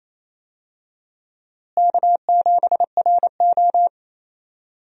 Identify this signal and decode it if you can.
Morse code, heard as a keyed tone, about 28 wpm, 700 Hz. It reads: K7RO
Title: USA Call Signs - 4 Characters - Repeated 2x - 28wpm